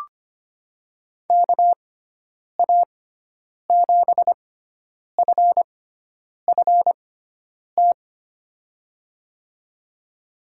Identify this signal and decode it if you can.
Morse code, heard as a keyed tone, about 25 wpm, spaced as for 9 wpm, 700 Hz. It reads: KA7FFT
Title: USA Call Signs - 6 Characters - 6x Character Spacing - 25wpm